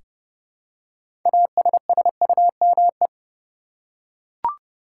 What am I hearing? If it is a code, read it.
Morse code, heard as a keyed tone, about 30 wpm, 700 Hz. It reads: ASSUME
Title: Six Letter Words - 30wpm